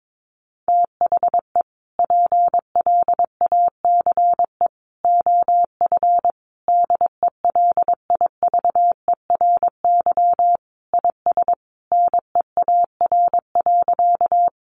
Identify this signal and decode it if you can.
Morse code, heard as a keyed tone, about 22 wpm, 700 Hz. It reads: THE PLACE OF DELIVERY IS NEAR.
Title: Sentences from Top 1500 Words 22wpm